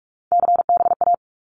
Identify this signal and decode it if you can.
Morse code, heard as a keyed tone, about 44 wpm, 700 Hz. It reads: CBA
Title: Abbreviations Repeated 3x - 45wpm